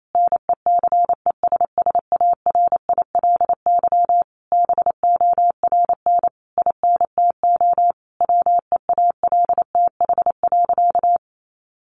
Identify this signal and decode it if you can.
Morse code, heard as a keyed tone, about 28 wpm, 700 Hz. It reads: NECESSARILY BORN INTO WEALTH.
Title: Sentences from Top 1400 Words 28wpm